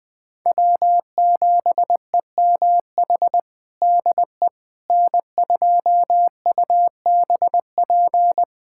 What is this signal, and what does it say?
Morse code, heard as a keyed tone, about 20 wpm, 700 Hz. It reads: W7EMH DE N2UBP
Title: USA Callsign DE Callsign - 5 Characters - 20wpm